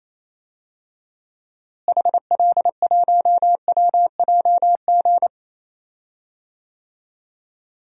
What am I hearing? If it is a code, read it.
Morse code, heard as a keyed tone, about 28 wpm, 700 Hz. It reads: HL1WJG
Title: DXCC-100 Call Signs - 6 Characters - Repeated 2x - 28wpm